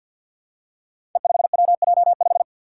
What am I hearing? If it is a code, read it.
Morse code, heard as a keyed tone, about 52 wpm, 700 Hz. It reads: E5WJ5
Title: Five Challenging Letter-Number Combinations 50wpm